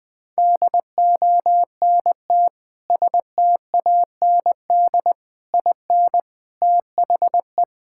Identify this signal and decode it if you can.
Morse code, heard as a keyed tone, about 20 wpm, 700 Hz. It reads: DONT STAND IN THE
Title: Sentences with 3-Word Phrases - 20wpm